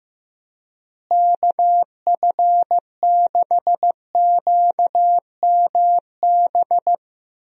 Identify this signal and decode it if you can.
Morse code, heard as a keyed tone, about 15 wpm, 700 Hz. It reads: KF6QMB
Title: USA Call Signs - 6 Characters - Repeated 3x - 15wpm